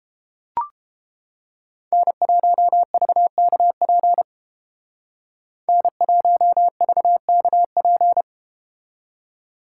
Morse code, heard as a keyed tone, about 32 wpm, 700 Hz. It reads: N1VKP N1VKP